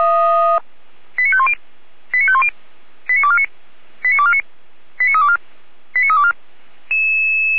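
Five-tone selective call (ZVEI), heard as quick runs of five tones, tones of about 70 ms. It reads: 89510 89510 89249 89249 89324 89324
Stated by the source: Five-tone ZVEI selective call received by SDR, from audio.f85360.feuerwehr-orlamuende.org fire brigade server.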